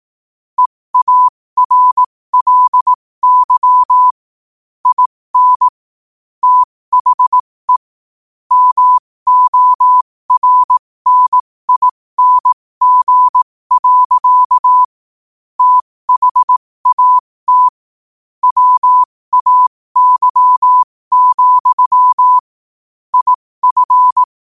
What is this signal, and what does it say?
Morse code, heard as a keyed tone, about 18 wpm, spaced as for 15 wpm, 1000 Hz. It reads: EARLY IN THE MORNING. THAT WAY, IF
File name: qotdfiles / 15wpm